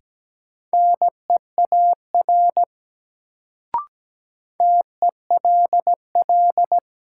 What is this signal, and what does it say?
Morse code, heard as a keyed tone, about 17 wpm, 700 Hz. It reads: NEAR TELL
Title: Sentences from Top 200 Words - New Word Drill - 17wpm